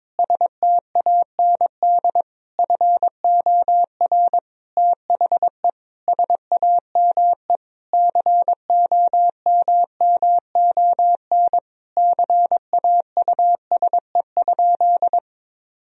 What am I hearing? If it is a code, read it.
Morse code, heard as a keyed tone, about 22 wpm, 700 Hz. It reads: STAND FOR THE SAME COMMON CAUSE?
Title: Sentences from Top 1000 Words - Review Part 3 - 22wpm